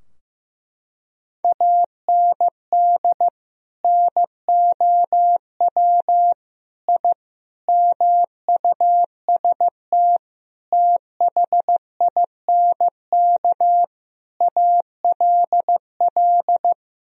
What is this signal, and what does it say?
Morse code, heard as a keyed tone, about 16 wpm, 700 Hz. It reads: AND NOW I MUST THINK ALL
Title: Sentences from Top 1000 Words - Review Part 3 - 15wpm